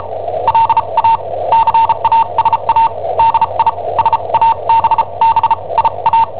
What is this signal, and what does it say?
Morse code, heard as a keyed tone, about 34 wpm, spaced as for 28 wpm, 910 Hz. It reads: LA CASA DI SABBIA
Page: Cassetta esercizi preparata da I7OHP per IW5ELP